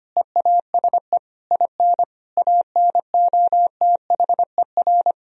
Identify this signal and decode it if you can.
Morse code, heard as a keyed tone, about 25 wpm, 700 Hz. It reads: EASE IN ANOTHER
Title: Sentences from Top 1000 Words - Review Part 1 - 25wpm